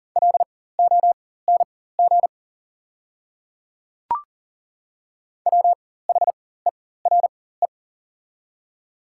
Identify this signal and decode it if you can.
Morse code, heard as a keyed tone, about 37 wpm, spaced as for 18 wpm, 700 Hz. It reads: LONG WHERE
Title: 4x Character Spacing - Top 300 Words - 40wpm